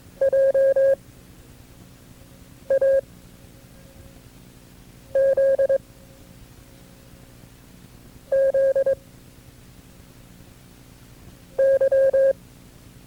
Morse code, heard as a keyed tone, about 22 wpm, 540 Hz. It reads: J A Z Z Y
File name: FreeFiles / CW-practice / KB6MT-tape2-side1